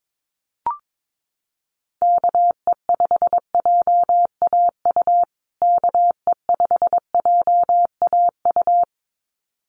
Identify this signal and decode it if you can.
Morse code, heard as a keyed tone, about 22 wpm, 700 Hz. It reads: KE5JAU KE5JAU
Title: USA Call Signs Repeated 2x 22wpm